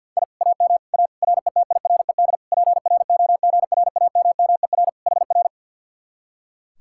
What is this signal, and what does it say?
Morse code, heard as a keyed tone, about 65 wpm, spaced as for 57 wpm, 700 Hz. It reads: I AM A RETIRED PROGRAMMER HR
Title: Speed-Racing Common QSO Phrases Encore - 50wpm